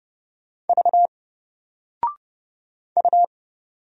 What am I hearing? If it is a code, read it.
Morse code, heard as a keyed tone, about 30 wpm, 700 Hz. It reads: V U
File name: Single-Letter-Number-30wpm